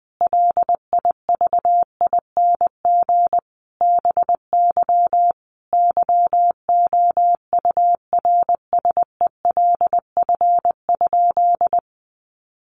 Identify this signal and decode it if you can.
Morse code, heard as a keyed tone, about 20 wpm, 700 Hz. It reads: LIVING BY YOURSELF?
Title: Sentences from Top 1100 Words 20wpm